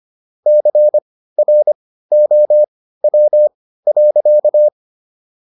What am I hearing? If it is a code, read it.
Morse code, heard as a keyed tone, about 25 wpm, spaced as for 15 wpm, 600 Hz. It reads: CROW.